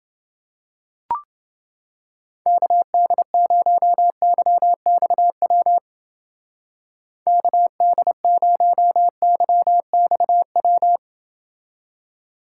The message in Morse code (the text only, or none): KD0YXW KD0YXW